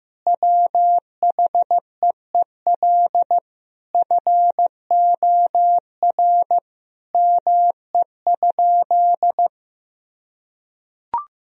Morse code, heard as a keyed tone, about 15 wpm, 700 Hz. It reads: WHEEL FOR ME?
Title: Sentences from Top 1000 Words - Review Part 3 - 15wpm